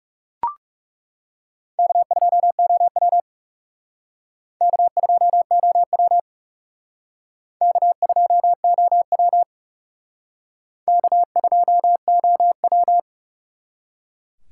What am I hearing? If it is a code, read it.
Morse code, heard as a keyed tone, about 37 wpm, 700 Hz. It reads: K2OW K2OW K2OW K2OW
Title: Speed Racing - USA 1x2 Call Signs 30wpm